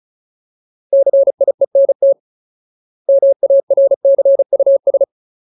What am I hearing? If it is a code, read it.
Morse code, heard as a keyed tone, about 34 wpm, 550 Hz. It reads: CIENT MARCUS
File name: Ultimate Word List / 35WPM Ultimate